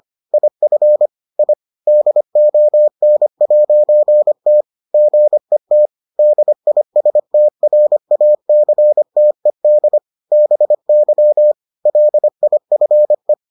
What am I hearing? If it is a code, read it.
Morse code, heard as a keyed tone, about 25 wpm, 600 Hz. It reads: IF I DON'T GET DISTRACTED BY LIFE